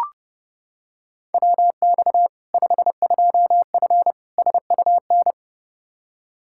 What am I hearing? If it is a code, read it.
Morse code, heard as a keyed tone, about 30 wpm, 700 Hz. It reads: WX 52F SUN
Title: Practice QSO - Exchange 2 - Info - 30wpm